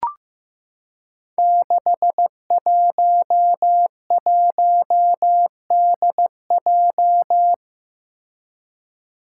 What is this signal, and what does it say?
Morse code, heard as a keyed tone, about 15 wpm, 700 Hz. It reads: 611DJ
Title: Five Challenging Letter-Number Combinations 15wpm